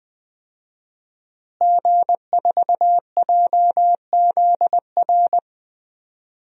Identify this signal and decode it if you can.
Morse code, heard as a keyed tone, about 20 wpm, 700 Hz. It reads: G4JZR
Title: DXCC-100 Call Signs - 5 Characters - Repeated 3x - 20wpm